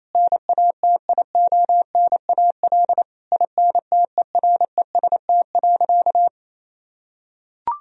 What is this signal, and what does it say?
Morse code, heard as a keyed tone, about 27 wpm, 700 Hz. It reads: NATIONAL INTEREST.
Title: Sentences from Top 1000 Words - Review Part 4 - 28wpm